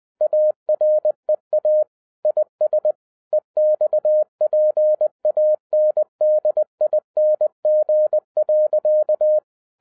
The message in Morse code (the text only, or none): AREA IS EXPANDING.